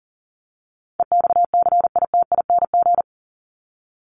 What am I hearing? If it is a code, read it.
Morse code, heard as a keyed tone, about 40 wpm, 700 Hz. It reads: EXCITING
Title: Top 1751-2000 Words 40wpm